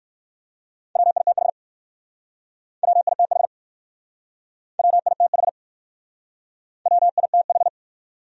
Morse code, heard as a keyed tone, about 60 wpm, 690 Hz. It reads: WITH WITH WITH WITH